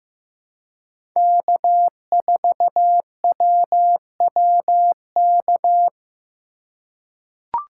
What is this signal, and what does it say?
Morse code, heard as a keyed tone, about 15 wpm, 700 Hz. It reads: K4WWK
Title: USA Call Signs - 5 Characters - 2x Character Spacing - 15wpm